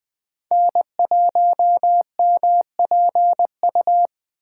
Morse code, heard as a keyed tone, about 20 wpm, 700 Hz. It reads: N1MPU
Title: USA Call Signs - 5 Characters - 8x Character Spacing - 20wpm